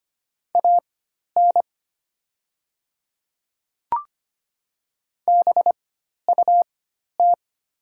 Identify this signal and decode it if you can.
Morse code, heard as a keyed tone, about 25 wpm, spaced as for 12 wpm, 690 Hz. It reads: AN BUT
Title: 4x Character Spacing - Top 100 Words - 25wpm